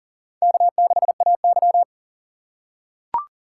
Morse code, heard as a keyed tone, about 44 wpm, 700 Hz. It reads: K6AY
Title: USA Call Signs - 4 Characters - 4x Character Spacing - 40wpm